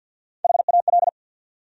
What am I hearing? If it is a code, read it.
Morse code, heard as a keyed tone, about 52 wpm, 700 Hz. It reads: HAL
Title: USA Common Names - 50wpm